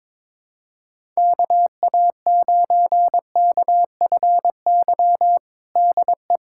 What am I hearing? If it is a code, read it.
Morse code, heard as a keyed tone, about 22 wpm, 700 Hz. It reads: KA9KFY DE
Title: USA Callsign DE Callsign - 6 Characters - 22wpm